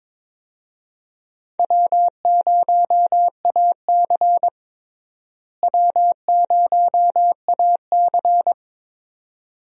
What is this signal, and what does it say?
Morse code, heard as a keyed tone, about 22 wpm, 700 Hz. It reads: W0AC W0AC
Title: USA Call Signs - 4 Characters - Repeated 2x - 22wpm